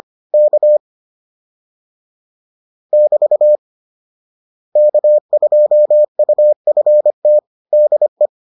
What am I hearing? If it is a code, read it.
Morse code, heard as a keyed tone, about 25 wpm, 600 Hz. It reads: K <BT> K2UFT DE